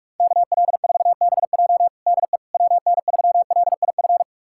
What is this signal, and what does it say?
Morse code, heard as a keyed tone, about 46 wpm, 700 Hz. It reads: KL4BJ DE WN3LIF
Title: USA Callsign DE Callsign 45wpm